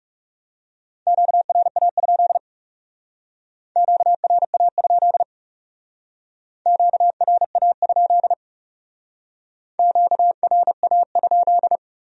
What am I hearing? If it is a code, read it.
Morse code, heard as a keyed tone, about 38 wpm, 700 Hz. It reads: QRA? QRA? QRA? QRA?